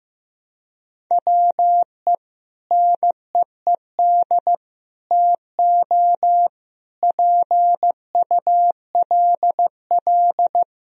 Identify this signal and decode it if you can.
Morse code, heard as a keyed tone, about 15 wpm, 700 Hz. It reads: WE NEED TO PULL
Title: Sentences from Top 1000 Words - Review Part 2 - 15wpm